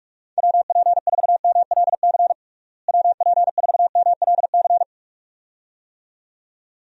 Morse code, heard as a keyed tone, about 46 wpm, 700 Hz. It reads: WP4MLC WP4MLC
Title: USA Call Signs - 6 Characters - Repeated 2x - 45wpm